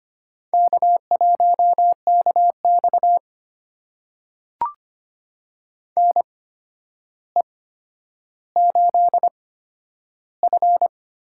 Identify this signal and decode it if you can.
Morse code, heard as a keyed tone, about 25 wpm, spaced as for 11 wpm, 700 Hz. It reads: K1KX NE8F
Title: USA Call Signs - 4 Characters - 8x Character Spacing - 25wpm